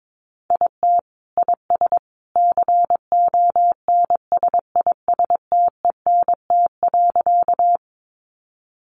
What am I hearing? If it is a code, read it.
Morse code, heard as a keyed tone, about 22 wpm, 700 Hz. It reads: IT IS CONSISTENT.
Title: Sentences from Top 1500 Words 22wpm